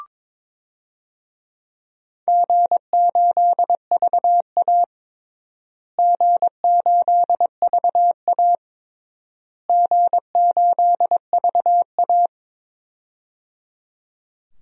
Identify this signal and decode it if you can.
Morse code, heard as a keyed tone, about 22 wpm, 700 Hz. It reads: G8VA G8VA G8VA
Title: DXCC-100 Call Signs - 4 Characters - Repeated 3x - 22wpm